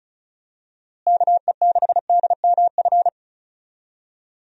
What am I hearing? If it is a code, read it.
Morse code, heard as a keyed tone, about 34 wpm, 700 Hz. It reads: KE6DMF
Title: USA Call Signs - 6 Characters - Repeated 3x - 35wpm